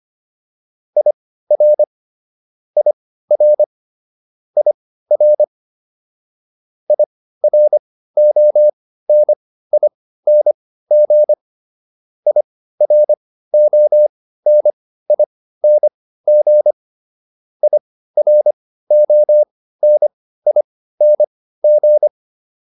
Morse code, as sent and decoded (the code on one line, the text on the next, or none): .. .-. / .. .-. / .. .-. / .. .-. --- -. .. -. --. / .. .-. --- -. .. -. --. / .. .-. --- -. .. -. --.
IR IR IR IRONING IRONING IRONING